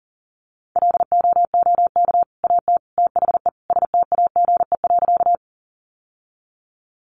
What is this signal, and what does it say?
Morse code, heard as a keyed tone, about 40 wpm, 700 Hz. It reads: LOOK AT THE STAGE.
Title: Sentences from Top 1200 Words 40wpm